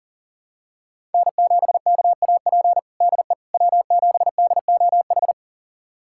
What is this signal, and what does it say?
Morse code, heard as a keyed tone, about 40 wpm, 700 Hz. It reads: N7KAP DE W7DOH
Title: USA Callsign DE Callsign - 5 Characters - 40wpm